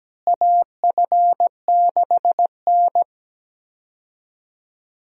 Morse code, heard as a keyed tone, about 17 wpm, 700 Hz. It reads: AF6N
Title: USA Call Signs - 4 Characters - Repeated 2x - 17wpm